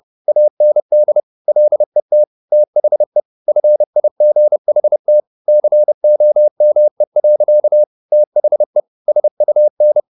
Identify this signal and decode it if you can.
Morse code, heard as a keyed tone, about 30 wpm, 600 Hz. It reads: AND LET THE FIGHT COME. THE SUN